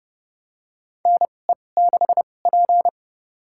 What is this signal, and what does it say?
Morse code, heard as a keyed tone, about 30 wpm, spaced as for 22 wpm, 700 Hz. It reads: NE6P
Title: USA Call Signs - 4 Characters - 2x Character Spacing - 30wpm